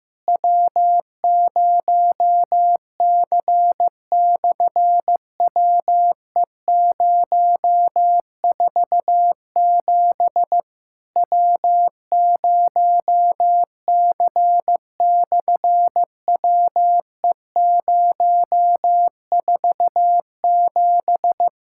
Morse code, heard as a keyed tone, about 15 wpm, 700 Hz. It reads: W0C/WE047 W0C/WE047
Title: SOTA QSO Practice 15wpm